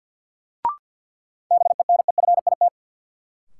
Morse code, heard as a keyed tone, about 50 wpm, 700 Hz. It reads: BENEFIT
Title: Sentences from Top 1000 Words - Review Part 6 - New Word Drill - 50wpm